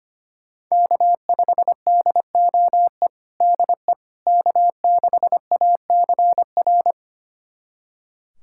Morse code, heard as a keyed tone, about 25 wpm, 700 Hz. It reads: K5DOE DE K6ACR
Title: USA Callsign DE Callsign - 5 Characters - 25wpm